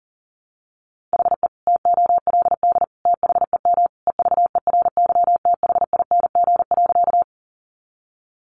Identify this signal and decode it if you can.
Morse code, heard as a keyed tone, about 40 wpm, 700 Hz. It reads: HE TOLD THEM EVERYTHING.
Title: Sentences from Top 700 Words 40wpm